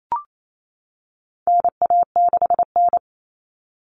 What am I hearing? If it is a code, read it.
Morse code, heard as a keyed tone, about 28 wpm, 700 Hz. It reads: NA6N
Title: Common Cut Numbers - 4-Digit 28wpm